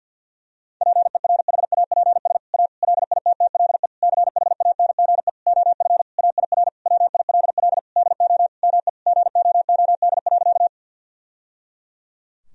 Morse code, heard as a keyed tone, about 52 wpm, 700 Hz. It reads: PERHAPS A LITTLE CHANGE OF AIR WILL DO ME GOOD.
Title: Sentences from Top 1000 Words - Review Part 1 - 50wpm